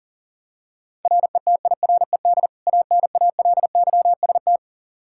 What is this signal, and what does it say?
Morse code, heard as a keyed tone, about 40 wpm, 700 Hz. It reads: RETIRED ANALYST
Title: Practice QSO - Exchange 3 - More Info - 40wpm